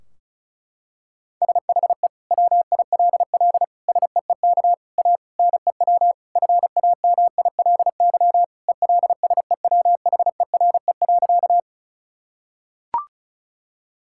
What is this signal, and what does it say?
Morse code, heard as a keyed tone, about 36 wpm, 700 Hz. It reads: SHE WILL SEEK A NEW FAMILY ELSEWHERE.
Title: Sentences from Top 1600 Words 35wpm